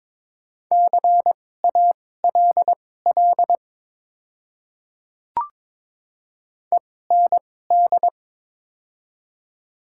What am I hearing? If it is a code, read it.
Morse code, heard as a keyed tone, about 22 wpm, spaced as for 16 wpm, 700 Hz. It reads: CALL END
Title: 2x Character Spacing - Top 200 Words - 22wpm